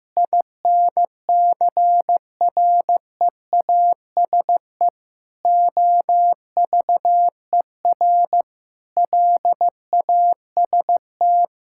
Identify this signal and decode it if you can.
Morse code, heard as a keyed tone, about 15 wpm, 700 Hz. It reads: INCREASE OVER LAST
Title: Sentences from Top 1000 Words - Review Part 5 - 15wpm